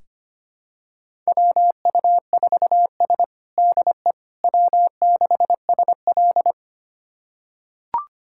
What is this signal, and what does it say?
Morse code, heard as a keyed tone, about 25 wpm, 700 Hz. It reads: WU4S DE W6SL
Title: USA Callsign DE Callsign - 4 Characters - 25wpm